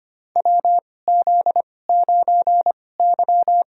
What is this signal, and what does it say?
Morse code, heard as a keyed tone, about 25 wpm, spaced as for 18 wpm, 700 Hz. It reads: WZ9Y
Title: USA Call Signs - 4 Characters - 2x Character Spacing - 25wpm